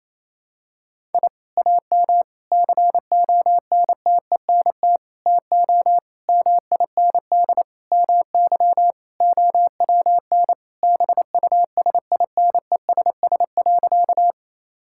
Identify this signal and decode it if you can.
Morse code, heard as a keyed tone, about 28 wpm, 700 Hz. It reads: I AM CONTENT TO MIND MY OWN BUSINESS.
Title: Sentences from Top 1000 Words - Review Part 6 - 28wpm